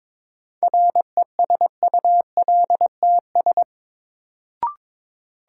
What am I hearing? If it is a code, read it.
Morse code, heard as a keyed tone, about 21 wpm, 700 Hz. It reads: RESULTS
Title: Sets of 4 Words - New Word Drill - 22wpm